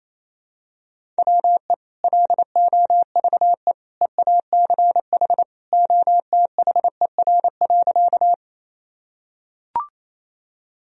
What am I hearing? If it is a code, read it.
Morse code, heard as a keyed tone, about 28 wpm, 700 Hz. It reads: WE LOVE EACH OTHER.
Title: Sentences from Top 1000 Words - Review Part 1 - 28wpm